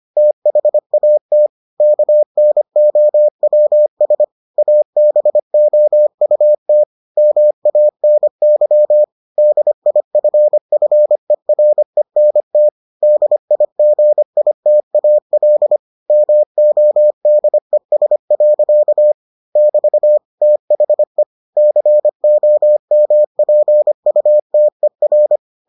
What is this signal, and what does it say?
Morse code, heard as a keyed tone, about 25 wpm, 600 Hz. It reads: THAT KNOWS ABOUT MANY DIFFERENT DIGITAL MODES. <BT> THE COMPUTER